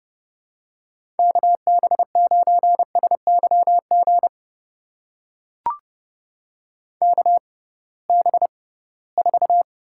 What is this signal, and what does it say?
Morse code, heard as a keyed tone, about 30 wpm, spaced as for 19 wpm, 700 Hz. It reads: KB9SYG KB4
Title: USA Call Signs - 6 Characters - 6x Character Spacing - 30wpm